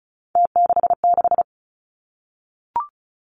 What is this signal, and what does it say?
Morse code, heard as a keyed tone, about 36 wpm, 700 Hz. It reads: T66
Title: Three Challenging Letter-Number Combinations 35wpm